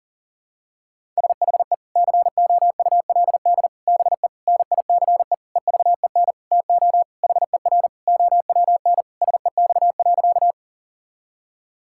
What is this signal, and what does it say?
Morse code, heard as a keyed tone, about 37 wpm, 700 Hz. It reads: SHE COULD BE NICE EVEN TO HER OWN SEX.
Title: Sentences from Top 1600 Words 40wpm